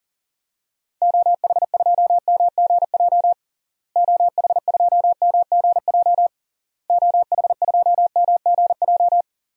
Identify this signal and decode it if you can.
Morse code, heard as a keyed tone, about 40 wpm, 700 Hz. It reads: OH2MGJ OH2MGJ OH2MGJ
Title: DXCC-100 Call Signs - 6 Characters - Repeated 3x - 40wpm